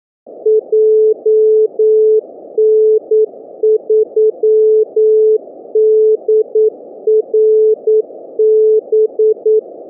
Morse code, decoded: JN3DRB